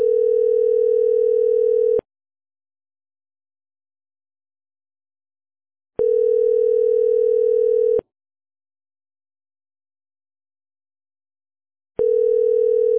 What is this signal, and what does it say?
Ringback tone (North America), 440 + 480 Hz.